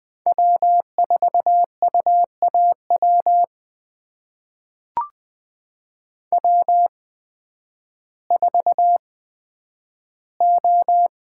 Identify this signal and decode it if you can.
Morse code, heard as a keyed tone, about 20 wpm, spaced as for 11 wpm, 700 Hz. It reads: W4UAW W4O
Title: USA Call Signs - 8x Character Spacing - 20wpm